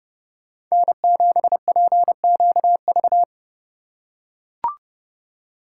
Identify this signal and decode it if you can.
Morse code, heard as a keyed tone, about 28 wpm, 700 Hz. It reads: N7PQV